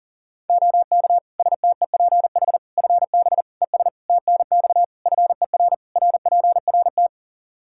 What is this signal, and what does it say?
Morse code, heard as a keyed tone, about 40 wpm, 700 Hz. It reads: OK STEPH FB ES TNX FER RPRT
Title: Practice QSO - Exchange 2 - Info - 40wpm